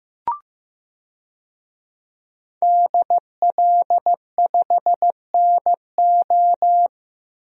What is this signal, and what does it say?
Morse code, heard as a keyed tone, about 15 wpm, 700 Hz. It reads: DL5NO
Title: DXCC-100 Call Signs - 5 Characters - Repeated 3x - 15wpm